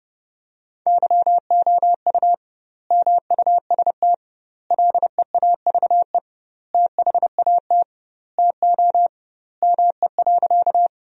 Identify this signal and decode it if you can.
Morse code, heard as a keyed tone, about 30 wpm, 700 Hz. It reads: YOU MUST LEAVE THAT TO ME.
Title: 2x Word-Spacing - Sentences from Top 400 Words - 30wpm